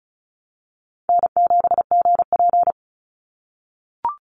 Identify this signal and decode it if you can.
Morse code, heard as a keyed tone, about 36 wpm, 700 Hz. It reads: N7GP
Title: USA Call Signs 35wpm